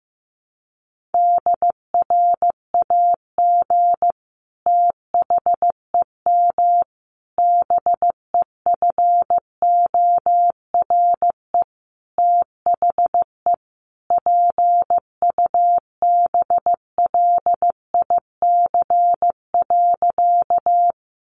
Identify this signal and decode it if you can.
Morse code, heard as a keyed tone, about 15 wpm, 700 Hz. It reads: DRAG THEM BEFORE THE PUBLIC.
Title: Sentences from Top 1500 Words 15wpm